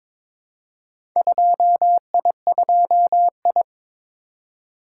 Morse code, heard as a keyed tone, about 23 wpm, 700 Hz. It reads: 2I2I
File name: practice / 22wpm / Four Letter-Number Combinations - 22wpm